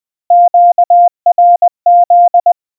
Morse code, heard as a keyed tone, about 20 wpm, 700 Hz. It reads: QRZ
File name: morse / qrz